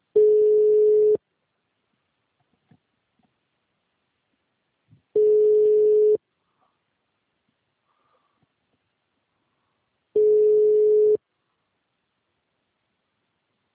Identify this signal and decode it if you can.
Ringback tone (Europe, 425 Hz).
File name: out- -2016- - - .34